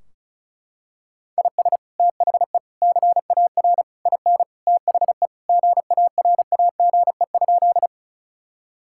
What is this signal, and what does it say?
Morse code, heard as a keyed tone, about 36 wpm, 700 Hz. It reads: IS THE CAR IN THE GARAGE?
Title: Sentences from Top 1600 Words 35wpm